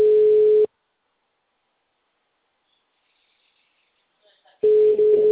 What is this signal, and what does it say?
Ringback tone (Europe, 425 Hz).
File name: out- -2056- - - .8310